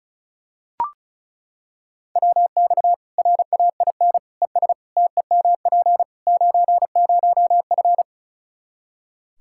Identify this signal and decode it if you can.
Morse code, heard as a keyed tone, about 36 wpm, 700 Hz. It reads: WX RAIN ES TEMP 90F
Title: Practice QSO - Exchange 2 - Info - 35wpm